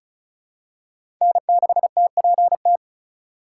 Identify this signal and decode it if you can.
Morse code, heard as a keyed tone, about 34 wpm, 700 Hz. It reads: N6TPT
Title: USA Call Signs - 5 Characters - 3x Character Spacing - 35wpm